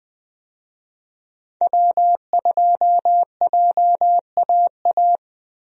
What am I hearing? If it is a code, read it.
Morse code, heard as a keyed tone, about 21 wpm, 700 Hz. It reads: W2JAA
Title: USA Call Signs - 5 Characters - Repeated 2x - 20wpm